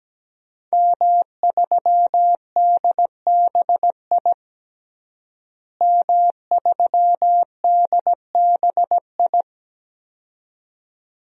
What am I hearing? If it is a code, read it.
Morse code, heard as a keyed tone, about 17 wpm, 700 Hz. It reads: M3DBI M3DBI
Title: DXCC-100 Call Signs - 5 Characters - Repeated 2x - 17wpm